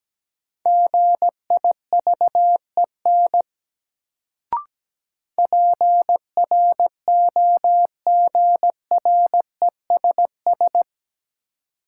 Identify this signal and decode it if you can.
Morse code, heard as a keyed tone, about 17 wpm, 700 Hz. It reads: GIVEN PROGRESS
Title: Sentences from Top 1400 Words - New Word Drill - 17wpm